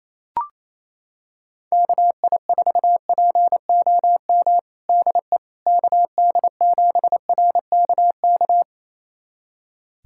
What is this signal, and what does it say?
Morse code, heard as a keyed tone, about 28 wpm, 700 Hz. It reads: KI4POM DE KD7RKK
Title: USA Callsign DE Callsign - 6 Characters - 28wpm